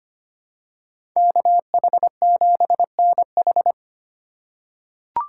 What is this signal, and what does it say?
Morse code, heard as a keyed tone, about 25 wpm, 700 Hz. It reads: KH7NH
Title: USA Call Signs - 5 Characters - 3x Character Spacing - 25wpm